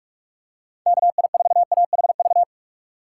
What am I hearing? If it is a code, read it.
Morse code, heard as a keyed tone, about 46 wpm, 700 Hz. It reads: KI4AHV
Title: USA Call Signs - 6 Characters - 4x Character Spacing - 45wpm